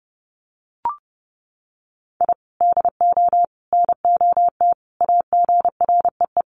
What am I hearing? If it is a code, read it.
Morse code, heard as a keyed tone, about 30 wpm, 700 Hz. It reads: I DO NOT AGREE
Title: Sets of 5 Words Encore 30wpm